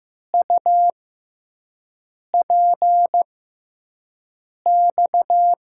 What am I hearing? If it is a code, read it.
Morse code, heard as a keyed tone, about 15 wpm, spaced as for 5 wpm, 700 Hz. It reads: UPX
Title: USA Call Signs - 6 Characters - 6x Character Spacing - 15wpm